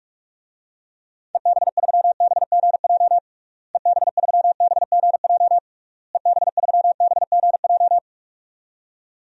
Morse code, heard as a keyed tone, about 41 wpm, 700 Hz. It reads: EB3BGJ EB3BGJ EB3BGJ
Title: DXCC-100 Call Signs - 6 Characters - Repeated 3x - 45wpm